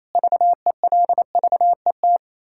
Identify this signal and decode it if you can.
Morse code, heard as a keyed tone, about 28 wpm, 700 Hz. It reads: VELVET
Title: Pangram Challenge v2 - 28wpm